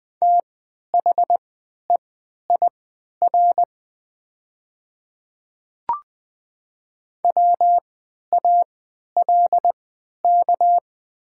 Morse code, heard as a keyed tone, about 20 wpm, spaced as for 11 wpm, 700 Hz. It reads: THEIR WALK
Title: 3x Character Spacing - Top 300 Words - 20wpm